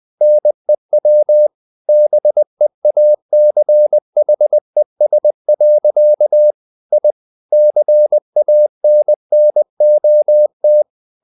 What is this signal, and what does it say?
Morse code, heard as a keyed tone, about 20 wpm, 600 Hz. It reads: NEW BEACHES. I CANNOT
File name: the old man and the sea part 2 20wpm